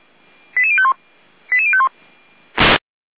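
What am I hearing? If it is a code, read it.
Five-tone selective call (ZVEI), heard as quick runs of five tones, tones of about 80 ms. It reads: 88051 88051
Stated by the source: ZVEI five-tone selective call received on an SDR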